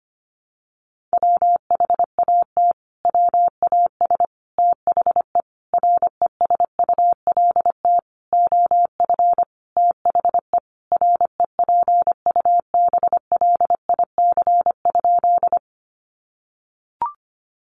Morse code, heard as a keyed tone, about 25 wpm, 700 Hz. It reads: WHAT WAS THE RESULT OF THE REPUBLIC?
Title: Sentences from Top 1500 Words 25wpm